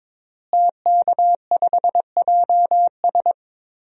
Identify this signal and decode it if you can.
Morse code, heard as a keyed tone, about 22 wpm, 700 Hz. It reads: TK5JS
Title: DXCC-100 Call Signs - 5 Characters - Repeated 3x - 22wpm